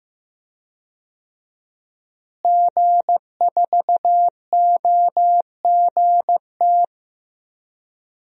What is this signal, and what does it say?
Morse code, heard as a keyed tone, about 15 wpm, 700 Hz. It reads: G4OGT
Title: DXCC-100 Call Signs - 5 Characters - Repeated 2x - 15wpm